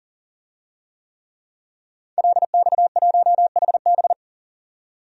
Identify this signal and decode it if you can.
Morse code, heard as a keyed tone, about 39 wpm, 700 Hz. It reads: LX1HB
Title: DXCC-100 Call Signs - 5 Characters - 40wpm